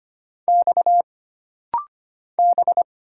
Morse code, heard as a keyed tone, about 25 wpm, 700 Hz. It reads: X B
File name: Single Character (Letter:Number:Punctuation:Prosign) - Rapid-Fire - 25wpm